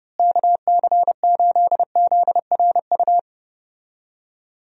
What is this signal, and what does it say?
Morse code, heard as a keyed tone, about 32 wpm, 700 Hz. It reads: KC8ZRU
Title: USA Call Signs - 6 Characters - Repeated 3x - 30wpm